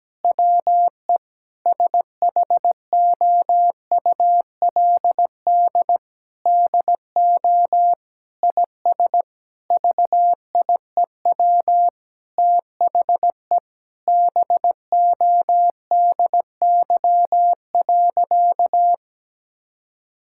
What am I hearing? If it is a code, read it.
Morse code, heard as a keyed tone, about 17 wpm, 700 Hz. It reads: WE SHOULD DO IS VIEW THE BODY.
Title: Sentences from Top 1000 Words - Review Part 6 - 17wpm